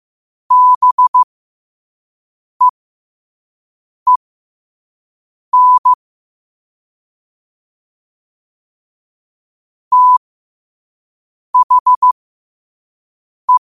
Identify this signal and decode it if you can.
Morse code, heard as a keyed tone, about 15 wpm, spaced as for 5 wpm, 1000 Hz. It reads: BEEN THE
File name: qotdfiles / 5wpm